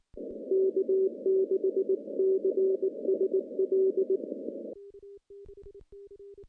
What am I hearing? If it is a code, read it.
Morse code, heard as a keyed tone, about 22 wpm, 380 Hz. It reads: K6CSL
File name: contests / CW-SS-k5tr- -0800